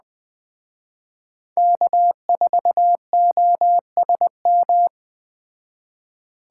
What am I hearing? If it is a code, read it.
Morse code, heard as a keyed tone, about 20 wpm, 700 Hz. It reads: K4OSM
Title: Speed Racing - USA 1x3 Call Signs 17wpm